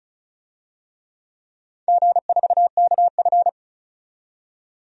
Morse code, heard as a keyed tone, about 36 wpm, 700 Hz. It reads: G4KF
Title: DXCC-100 Call Signs - 4 Characters - 35wpm